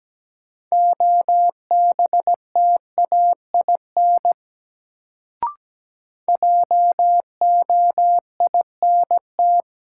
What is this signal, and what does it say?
Morse code, heard as a keyed tone, about 17 wpm, 700 Hz. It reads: OBTAIN JOINT
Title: Sentences from Top 1300 Words - New Word Drill - 17wpm